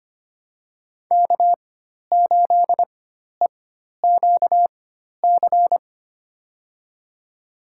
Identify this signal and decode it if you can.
Morse code, heard as a keyed tone, about 25 wpm, spaced as for 12 wpm, 700 Hz. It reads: K8EQC
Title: USA Call Signs - 5 Characters - 4x Character Spacing - 25wpm